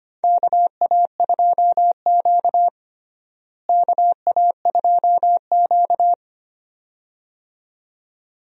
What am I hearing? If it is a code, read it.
Morse code, heard as a keyed tone, about 25 wpm, 700 Hz. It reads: KA2Q KA2Q
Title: USA Call Signs - 4 Characters - Repeated 2x - 25wpm